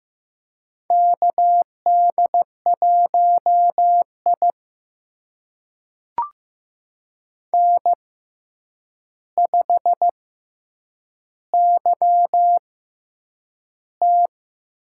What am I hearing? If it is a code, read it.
Morse code, heard as a keyed tone, about 15 wpm, spaced as for 8 wpm, 700 Hz. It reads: KD1I N5YT
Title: USA Call Signs - 4 Characters - 6x Character Spacing - 15wpm